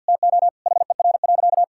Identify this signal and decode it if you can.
Morse code, heard as a keyed tone, about 51 wpm, 700 Hz. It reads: TO HER.
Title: Sentences from Top 1000 Words - Review Part 6 - 50wpm